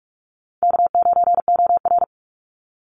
Morse code, heard as a keyed tone, about 44 wpm, 700 Hz. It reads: K9OR
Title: Easy Call Signs Repeated 2x - 45wpm